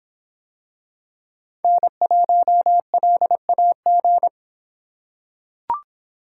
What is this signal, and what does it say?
Morse code, heard as a keyed tone, about 24 wpm, 700 Hz. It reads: N1LAG